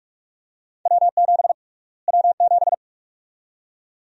Morse code, heard as a keyed tone, about 46 wpm, 700 Hz. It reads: W7 W7
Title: USA Call Sign Prefixes - Repeated 2x - 45wpm